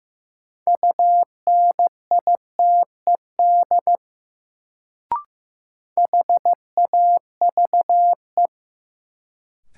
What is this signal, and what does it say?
Morse code, heard as a keyed tone, about 15 wpm, 700 Hz. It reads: UNITED HAVE
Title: Sets of 2 Words - New Word Drill - 15wpm